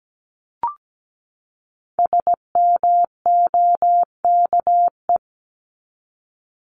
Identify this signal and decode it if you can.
Morse code, heard as a keyed tone, about 17 wpm, 700 Hz. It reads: SMOKE
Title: Top 1501-1750 Words 17wpm